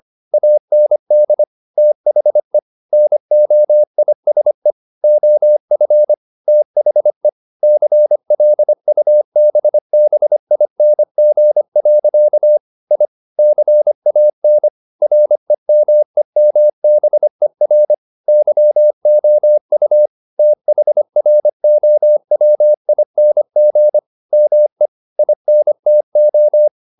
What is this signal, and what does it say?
Morse code, heard as a keyed tone, about 25 wpm, 600 Hz. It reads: AND THE NOISE OF THE CLUBBING. I CAN REMEMBER YOU THROWING ME INTO
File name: the old man and the sea part 1 25wpm